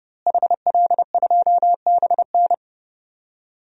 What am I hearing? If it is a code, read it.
Morse code, heard as a keyed tone, about 31 wpm, 700 Hz. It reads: HL2BN
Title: DXCC-100 Call Signs - 5 Characters - Repeated 2x - 30wpm